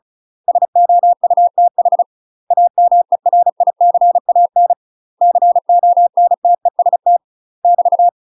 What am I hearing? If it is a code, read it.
Morse code, heard as a keyed tone, about 34 wpm, 700 Hz. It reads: SOUTH AMERICAN CONTEST <BT>
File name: UK MASTER QSOs TRAINING / 35WPM / UK MASTER QSOs 0021